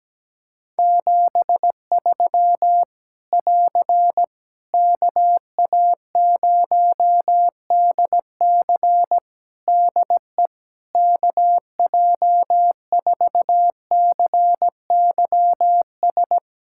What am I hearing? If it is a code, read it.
Morse code, heard as a keyed tone, about 17 wpm, 700 Hz. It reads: 73 <AR> KA0DC DE KJ4CYS
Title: K1USN SST QSO Practice 17wpm